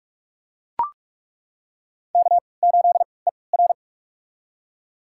Morse code, heard as a keyed tone, about 41 wpm, spaced as for 24 wpm, 690 Hz. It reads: K8ER
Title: USA Call Signs - 3x Character Spacing - 45wpm